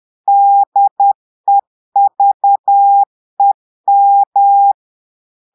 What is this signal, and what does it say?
Morse code, heard as a keyed tone, about 10 wpm, 800 Hz. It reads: DEVEM
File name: portugais texte en clair 2 vitesse 10 mots minutes